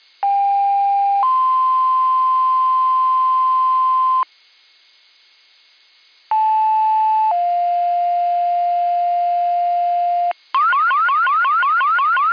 Two-tone page, heard as a one-second tone, then a three-second tone, about 800 then 1040 Hz, about 870 then 700 Hz, with noise. It is heard from 0.2 to 10.3 s.